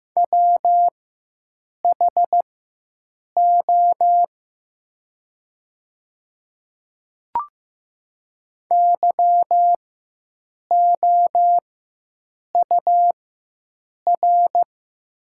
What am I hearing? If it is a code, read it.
Morse code, heard as a keyed tone, about 15 wpm, spaced as for 7 wpm, 700 Hz. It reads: WHO YOUR
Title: 4x Character Spacing - Top 100 Words - 15wpm